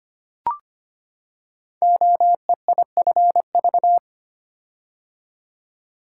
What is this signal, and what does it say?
Morse code, heard as a keyed tone, about 25 wpm, 700 Hz. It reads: OEIFV
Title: Code Groups - 5 Letters - 25wpm